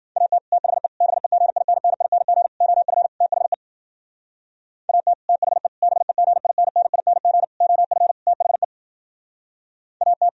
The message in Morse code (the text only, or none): AT THE BEGINNING OF THE AT THE BEGINNING OF THE AT